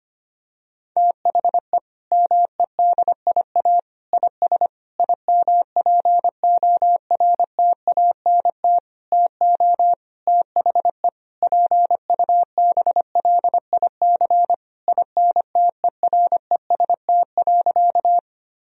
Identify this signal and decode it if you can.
Morse code, heard as a keyed tone, about 25 wpm, 700 Hz. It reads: THE MEDIA IS IMPORTANT TO THE PUBLIC INTEREST.
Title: Sentences from Top 1000 Words - Review Part 4 - 25wpm